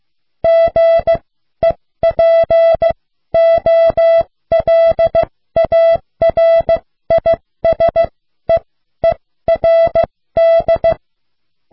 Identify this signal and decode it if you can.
Morse code, heard as a keyed tone, about 15 wpm, spaced as for 12 wpm, 660 Hz. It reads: GEPOLARISEERD